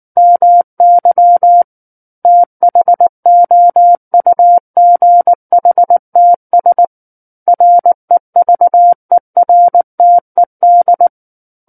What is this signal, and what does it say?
Morse code, heard as a keyed tone, about 19 wpm, 700 Hz. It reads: MY THOUGHTS REVERTED
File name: Book One / 10-Chapter 10 - 19 WPM